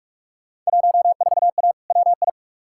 Morse code, heard as a keyed tone, about 46 wpm, 700 Hz. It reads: 14A WI
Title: Field Day Exchanges Encore - 45wpm